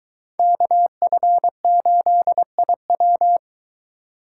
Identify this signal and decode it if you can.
Morse code, heard as a keyed tone, about 22 wpm, 700 Hz. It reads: KF8IW